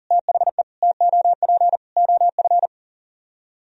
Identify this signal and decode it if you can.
Morse code, heard as a keyed tone, about 51 wpm, 700 Hz. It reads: THE TOP OF